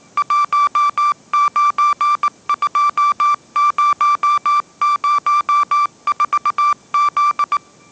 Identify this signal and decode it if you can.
Morse code, heard as a keyed tone, about 19 wpm, 1170 Hz. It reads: 192004Z
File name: 20wpm